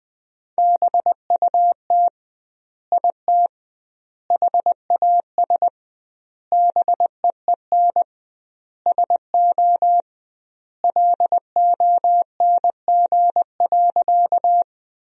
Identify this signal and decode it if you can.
Morse code, heard as a keyed tone, about 20 wpm, 700 Hz. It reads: BUT IT HAS BEEN SO LONG.
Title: 2x Word-Spacing - Sentences from Top 100 Words - 20wpm